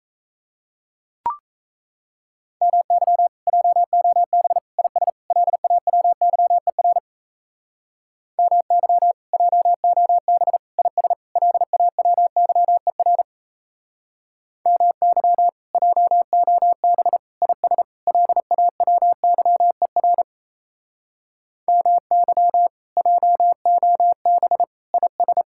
Morse code, heard as a keyed tone, about 37 wpm, 700 Hz. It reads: MY JOB IS LAWYER MY JOB IS LAWYER MY JOB IS LAWYER MY JOB IS